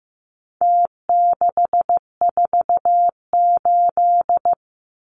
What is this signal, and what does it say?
Morse code, heard as a keyed tone, about 15 wpm, 700 Hz. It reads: T648
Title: Numbers - Contest Style - 4-Digit 15wpm